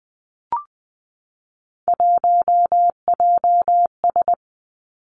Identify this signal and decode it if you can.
Morse code, heard as a keyed tone, about 20 wpm, 700 Hz. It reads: 1JS
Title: Three Challenging Letter-Number Combinations 20wpm